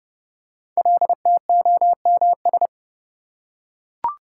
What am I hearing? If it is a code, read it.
Morse code, heard as a keyed tone, about 30 wpm, 700 Hz. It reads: LTOMS
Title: Code Groups - 5 Letter:Numbers - 30wpm